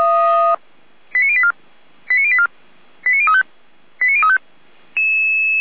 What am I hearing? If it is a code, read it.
Five-tone selective call (ZVEI), heard as quick runs of five tones, tones of about 70 ms. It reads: 89084 89084 89035 89035